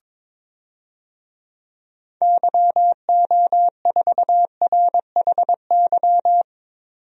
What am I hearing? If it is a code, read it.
Morse code, heard as a keyed tone, about 22 wpm, 700 Hz. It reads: YO4RHY
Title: DXCC-100 Call Signs - 6 Characters - 22wpm